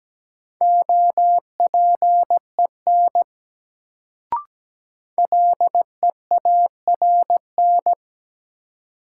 Morse code, heard as a keyed tone, about 17 wpm, 700 Hz. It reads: OPEN LEARN
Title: Sentences from Top 300 Words - New Word Drill - 17wpm